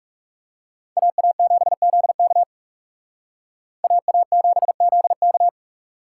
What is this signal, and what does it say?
Morse code, heard as a keyed tone, about 36 wpm, 700 Hz. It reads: AA7ZK AA7ZK
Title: Speed Racing - USA Call Signs 30wpm